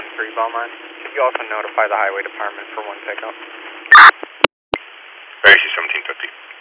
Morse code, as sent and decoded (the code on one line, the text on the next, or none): none